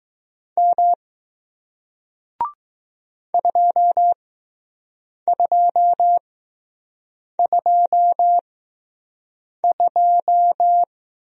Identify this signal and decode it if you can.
Morse code, heard as a keyed tone, about 20 wpm, spaced as for 6 wpm, 700 Hz. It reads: M 2222